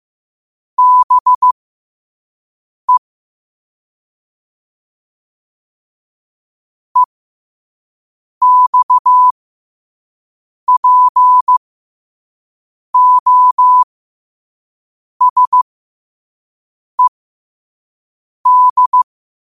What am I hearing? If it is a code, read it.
Morse code, heard as a keyed tone, about 15 wpm, spaced as for 5 wpm, 1000 Hz. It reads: BE EXPOSED
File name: qotdfiles / 5wpm